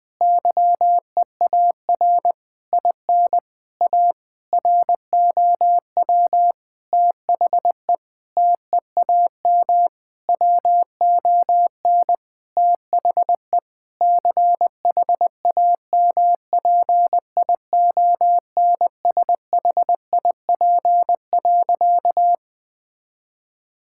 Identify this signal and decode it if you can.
Morse code, heard as a keyed tone, about 20 wpm, 700 Hz. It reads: YEAR IN A ROW THE TEAM WON THE CHAMPIONSHIP.
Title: Sentences with 8-Word Phrases - 20wpm